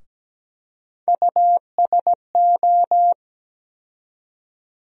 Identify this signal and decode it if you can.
Morse code, heard as a keyed tone, about 17 wpm, 700 Hz. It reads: USO
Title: USA Call Sign Postfixes - 3 Letters 17wpm